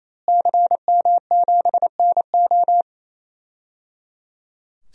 Morse code, heard as a keyed tone, about 28 wpm, 700 Hz. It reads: CM7NO
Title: DXCC-100 Call Signs - 5 Characters - Repeated 3x - 28wpm